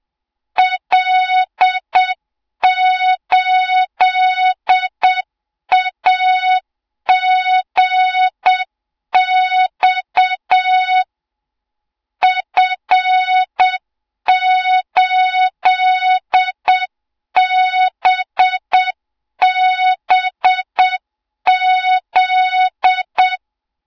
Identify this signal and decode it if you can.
Morse code, heard as a keyed tone, about 7 wpm, 3700 Hz, with noise. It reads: L8AGX F8BBZ
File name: Morze-HA2MN / hang / 20 GYAKORLAT